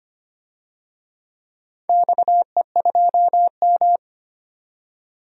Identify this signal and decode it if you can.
Morse code, heard as a keyed tone, about 25 wpm, 700 Hz. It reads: XE2M
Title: DXCC-100 Call Signs - 4 Characters - 25wpm